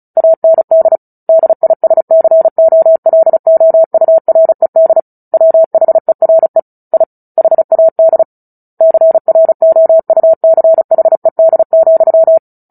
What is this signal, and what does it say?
Morse code, heard as a keyed tone, about 35 wpm, 650 Hz. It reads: AND DISCOLOURED WHERE I HAD CROUCHED,
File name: Book Two / 09-Chapter 9 - 35 WPM